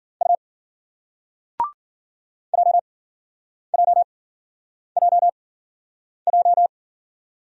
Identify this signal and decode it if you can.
Morse code, heard as a keyed tone, about 54 wpm, spaced as for 9 wpm, 700 Hz. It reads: U JJJJ